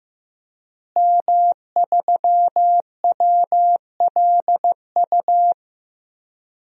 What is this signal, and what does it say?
Morse code, heard as a keyed tone, about 16 wpm, 700 Hz. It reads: M3WLU
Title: DXCC-100 Call Signs - 5 Characters - Repeated 3x - 15wpm